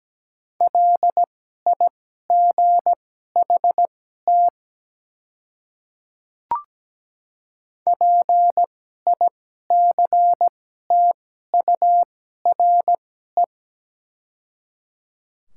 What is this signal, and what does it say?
Morse code, heard as a keyed tone, about 17 wpm, spaced as for 12 wpm, 690 Hz. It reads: LIGHT PICTURE
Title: 2x Character Spacing - Top 300 Words - 17wpm